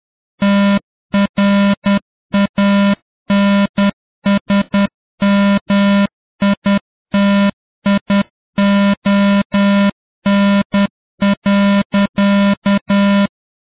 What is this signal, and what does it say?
Morse code, heard as a keyed tone, about 10 wpm, 200 Hz, with noise. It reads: TRANSMITION.